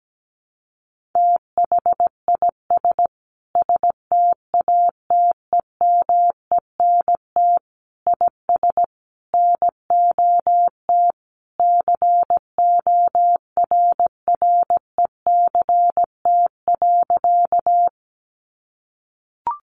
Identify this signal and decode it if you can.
Morse code, heard as a keyed tone, about 17 wpm, 700 Hz. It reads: THIS STATEMENT IS NOT CORRECT.
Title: Sentences from Top 1200 Words 17wpm